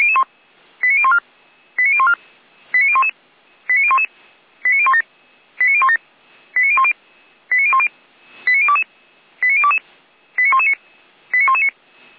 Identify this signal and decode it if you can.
Five-tone selective call (ZVEI), heard as quick runs of five tones, tones of about 70 ms. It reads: 89014 89014 89011 89011 89017 89017 89010 89010 89022 89022 89119 89119